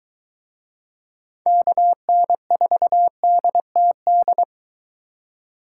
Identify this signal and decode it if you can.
Morse code, heard as a keyed tone, about 23 wpm, 700 Hz. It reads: KN4DTD